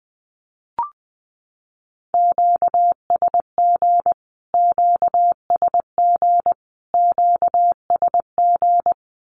Morse code, heard as a keyed tone, about 20 wpm, 700 Hz. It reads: QSG QSG QSG
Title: Q-Codes Repeated 3x 20wpm